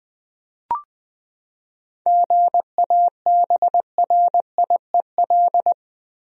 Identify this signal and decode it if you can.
Morse code, heard as a keyed tone, about 20 wpm, 700 Hz. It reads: GABRIEL
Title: USA Common Names - 20wpm